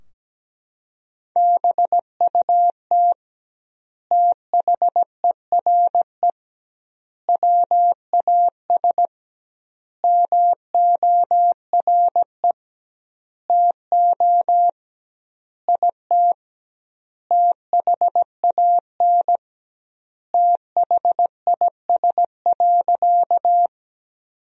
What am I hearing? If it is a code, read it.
Morse code, heard as a keyed tone, about 17 wpm, 700 Hz. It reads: BUT THERE WAS MORE TO IT THAN THIS.
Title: 2x Word-Spacing - Sentences from Top 100 Words - 17wpm